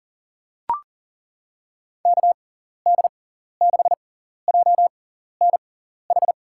Morse code, heard as a keyed tone, about 37 wpm, spaced as for 14 wpm, 700 Hz. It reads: KD6JNH
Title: USA Call Signs - 6x Character Spacing - 40wpm